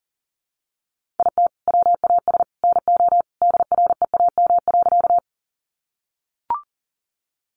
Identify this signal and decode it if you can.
Morse code, heard as a keyed tone, about 40 wpm, 700 Hz. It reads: IT WAS NO DREAM.
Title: Sentences from Top 1500 Words 40wpm